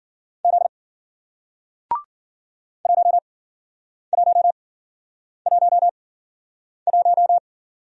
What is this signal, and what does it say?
Morse code, heard as a keyed tone, about 52 wpm, spaced as for 9 wpm, 700 Hz. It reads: Z 1111